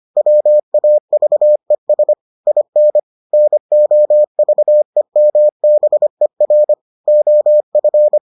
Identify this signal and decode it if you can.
Morse code, heard as a keyed tone, about 25 wpm, 600 Hz. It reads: WAVES IN NOVEMBER OF